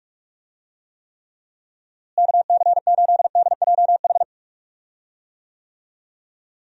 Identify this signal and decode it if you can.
Morse code, heard as a keyed tone, about 46 wpm, 700 Hz. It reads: KC8DJH
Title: USA Call Signs - 6 Characters - Repeated 3x - 45wpm